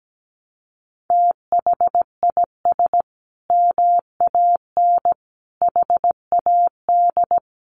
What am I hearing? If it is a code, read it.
Morse code, heard as a keyed tone, about 17 wpm, 700 Hz. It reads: THIS MAN HAD
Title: Sentences from Top 1200 Words 17wpm